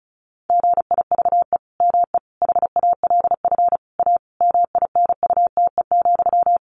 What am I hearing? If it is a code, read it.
Morse code, heard as a keyed tone, about 34 wpm, 700 Hz. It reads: GIVE ME HALF A MINUTE,
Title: Sentences from Top 1500 Words 35wpm